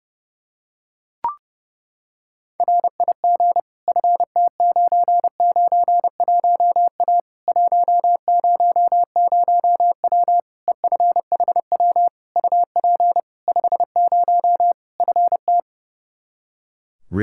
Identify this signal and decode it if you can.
Morse code, heard as a keyed tone, about 32 wpm, 700 Hz. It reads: RIG FT991A 100W EFHW UP 50 FT
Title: Practice QSO - Exchange 2 - Info - 30wpm